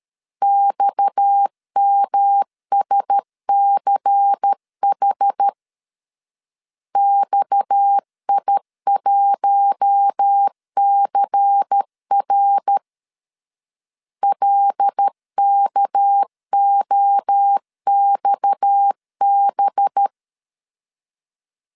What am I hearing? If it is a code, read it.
Morse code, heard as a keyed tone, about 13 wpm, 800 Hz. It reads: XMSCH XI1CR LKOXB
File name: LEZIONI / 6 PKX0 / 6 PKX 12wpm 2024